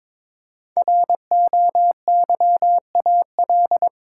Morse code, heard as a keyed tone, about 22 wpm, 700 Hz. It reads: ROYAL
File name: 5 Letter Words - First 3 Letters Given - 22wpm